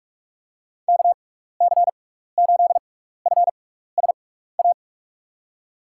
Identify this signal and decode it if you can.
Morse code, heard as a keyed tone, about 45 wpm, spaced as for 15 wpm, 700 Hz. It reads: KC8FSA
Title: USA Call Signs - 6 Characters - 6x Character Spacing - 45wpm